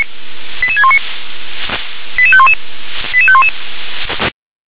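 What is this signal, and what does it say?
Five-tone selective call (ZVEI), heard as quick runs of five tones, tones of about 70 ms. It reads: 88619 88511 88511
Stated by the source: ZVEI five-tone selective call 88850 received by SDR.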